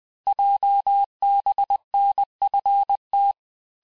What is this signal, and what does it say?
Morse code, heard as a keyed tone, about 20 wpm, 800 Hz. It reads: JBNFT
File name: cours-858-texte-chiffre-vitesse-19mpm